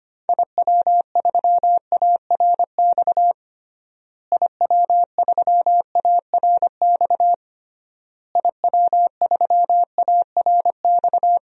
Morse code, heard as a keyed tone, about 25 wpm, 700 Hz. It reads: IW3ARX IW3ARX IW3ARX
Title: DXCC-100 Call Signs - 6 Characters - Repeated 3x - 25wpm